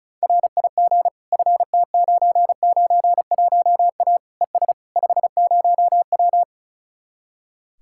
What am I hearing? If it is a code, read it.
Morse code, heard as a keyed tone, about 44 wpm, 700 Hz. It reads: RIG FT991A ES 50W